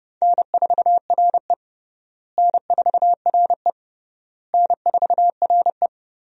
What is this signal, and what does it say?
Morse code, heard as a keyed tone, about 30 wpm, 700 Hz. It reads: N4RE N4RE N4RE
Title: USA Call Signs - 4 Characters - Repeated 3x - 30wpm